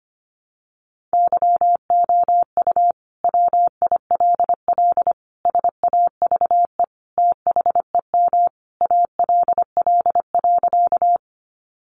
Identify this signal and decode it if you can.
Morse code, heard as a keyed tone, about 25 wpm, 700 Hz. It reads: YOU WILL SAVE THEM ALL.
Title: Sentences from Top 1100 Words 25wpm